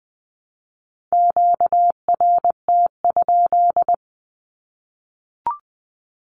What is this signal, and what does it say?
Morse code, heard as a keyed tone, about 20 wpm, 700 Hz. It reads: QRT?
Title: Q-Codes Repeated 3x 20wpm